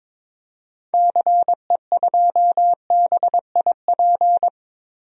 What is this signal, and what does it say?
Morse code, heard as a keyed tone, about 21 wpm, 700 Hz. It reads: CE2BIP
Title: DXCC-100 Call Signs - 6 Characters - 22wpm